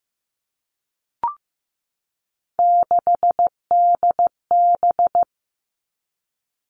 Morse code, heard as a keyed tone, about 15 wpm, 700 Hz. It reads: 6DB
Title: Three Challenging Letter-Number Combinations 15wpm